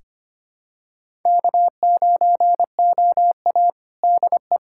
Morse code, heard as a keyed tone, about 25 wpm, 700 Hz. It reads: K9OA DE
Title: USA Callsign DE Callsign - 4 Characters - 25wpm